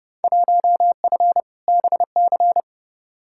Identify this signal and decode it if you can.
Morse code, heard as a keyed tone, about 30 wpm, 700 Hz. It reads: 1F BC
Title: Field Day Exchanges Encore - 30wpm